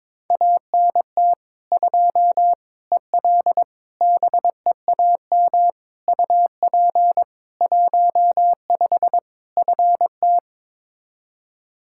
Morse code, heard as a keyed tone, about 22 wpm, 700 Hz. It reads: ANT 2 EL BEAM UP 15 FT
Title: Practice QSO - Exchange 2 - Info - 22wpm